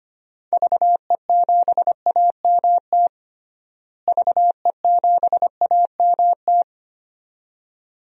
Morse code, heard as a keyed tone, about 25 wpm, 700 Hz. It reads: VE7AMT VE7AMT
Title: DXCC-100 Call Signs - 6 Characters - Repeated 2x - 25wpm